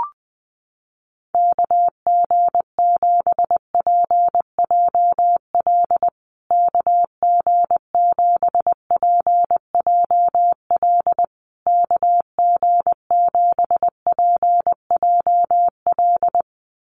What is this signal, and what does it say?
Morse code, heard as a keyed tone, about 20 wpm, 700 Hz. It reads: KG7PJL KG7PJL KG7PJL
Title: USA Call Signs Repeated 3x 20wpm